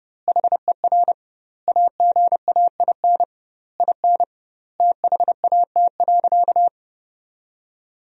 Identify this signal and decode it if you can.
Morse code, heard as a keyed tone, about 32 wpm, 700 Hz. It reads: HER AGAIN IN THAT.
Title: 2x Word-Spacing - Sentences from Top 200 Words - 30wpm